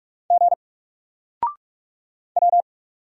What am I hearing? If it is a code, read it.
Morse code, heard as a keyed tone, about 45 wpm, 700 Hz. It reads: G W
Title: Single Letter-Number Alt 45wpm